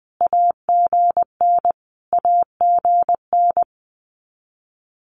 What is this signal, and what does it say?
Morse code, heard as a keyed tone, about 20 wpm, 700 Hz. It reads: AGN AGN
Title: Abbreviations Repeated 3x - 20wpm